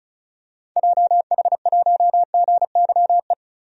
Morse code, heard as a keyed tone, about 34 wpm, 700 Hz. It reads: JH1GYE
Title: DXCC-100 Call Signs - 6 Characters - 35wpm